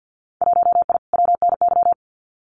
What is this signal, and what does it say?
Morse code, heard as a keyed tone, about 50 wpm, 700 Hz. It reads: 1I WNY
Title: Winter Field Day Exchanges Encore 50wpm